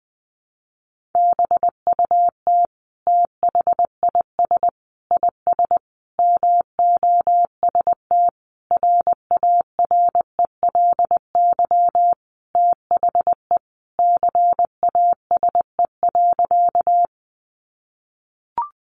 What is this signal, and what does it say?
Morse code, heard as a keyed tone, about 20 wpm, 700 Hz. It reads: BUT THIS IS MOST RARELY THE CASE.
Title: Sentences from Top 1400 Words 20wpm